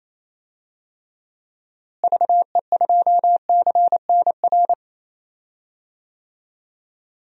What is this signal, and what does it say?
Morse code, heard as a keyed tone, about 28 wpm, 700 Hz. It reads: VE2CNR
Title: DXCC-100 Call Signs - 6 Characters - Repeated 3x - 28wpm